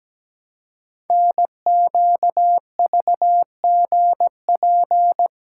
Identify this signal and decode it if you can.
Morse code, heard as a keyed tone, about 17 wpm, 700 Hz. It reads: NQVGP
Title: Code Groups - 5 Letters - 17wpm